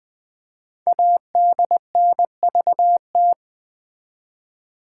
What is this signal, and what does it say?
Morse code, heard as a keyed tone, about 20 wpm, 700 Hz. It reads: ADNVT
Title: Code Groups - 5 Letters - 20wpm